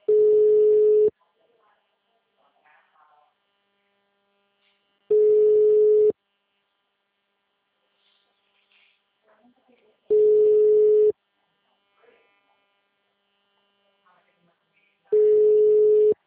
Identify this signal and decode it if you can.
Ringback tone (Europe, 425 Hz).
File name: out- -2012- - - .41217